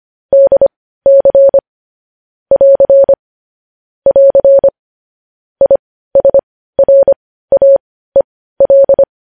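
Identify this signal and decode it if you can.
Morse code, heard as a keyed tone, about 25 wpm, spaced as for 15 wpm, 550 Hz. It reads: DC <AR> <AR> ISRAEL
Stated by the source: Morse code news headlines sent at 15.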